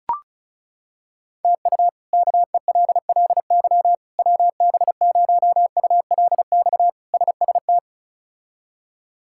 Morse code, heard as a keyed tone, about 36 wpm, 700 Hz. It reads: TU KELLY WB0ULX SST
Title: K1USN SST QSO Practice - 35wpm